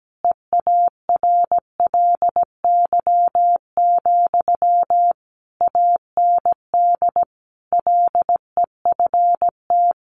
Morse code, heard as a keyed tone, about 17 wpm, 700 Hz. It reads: EARLY, AND LEFT
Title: Sentences from Top 400 Words 17wpm